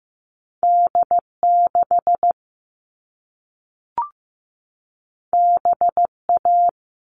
Morse code, heard as a keyed tone, about 15 wpm, 700 Hz. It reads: D6 BA
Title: All Cut Numbers - 2-Digit 15wpm